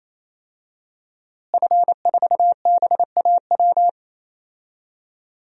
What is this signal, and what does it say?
Morse code, heard as a keyed tone, about 28 wpm, 700 Hz. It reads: F4BAW
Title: DXCC-100 Call Signs - 5 Characters - Repeated 3x - 28wpm